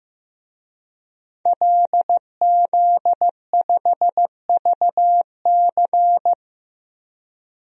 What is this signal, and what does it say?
Morse code, heard as a keyed tone, about 15 wpm, 700 Hz. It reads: LZ5VC
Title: DXCC-100 Call Signs - 5 Characters - Repeated 3x - 15wpm